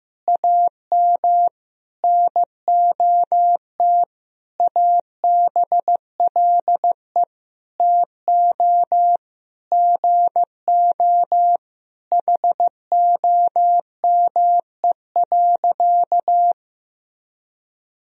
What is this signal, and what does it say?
Morse code, heard as a keyed tone, about 15 wpm, 700 Hz. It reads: AM NOT ABLE TO GO HOME.
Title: Sentences from Top 1000 Words - Review Part 3 - 15wpm